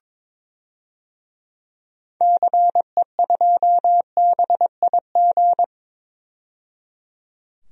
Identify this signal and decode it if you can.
Morse code, heard as a keyed tone, about 21 wpm, 700 Hz. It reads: CE2BIG
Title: DXCC-100 Call Signs - 6 Characters - 22wpm